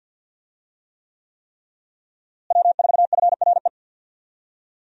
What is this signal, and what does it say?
Morse code, heard as a keyed tone, about 49 wpm, 700 Hz. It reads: W4FRE
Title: USA Call Signs - 5 Characters - Repeated 3x - 50wpm